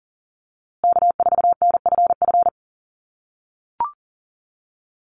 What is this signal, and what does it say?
Morse code, heard as a keyed tone, about 40 wpm, 700 Hz. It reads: K4NFF
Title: USA Call Signs Repeated 2x 40wpm